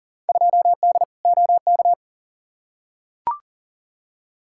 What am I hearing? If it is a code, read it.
Morse code, heard as a keyed tone, about 40 wpm, 700 Hz. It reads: 2D OK
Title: Field Day Exchanges Encore - 40wpm